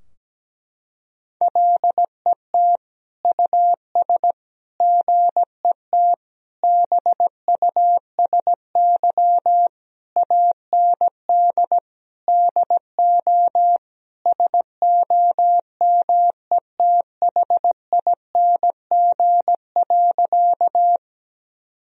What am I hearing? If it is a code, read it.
Morse code, heard as a keyed tone, about 17 wpm, 700 Hz. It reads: LET US GET BUSY AND DO SOMETHING.
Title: Sentences from Top 1000 Words - Review Part 3 - 17wpm